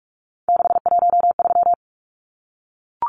Morse code, heard as a keyed tone, about 44 wpm, 700 Hz. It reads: 613
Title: Numbers - 3-Digit 45wpm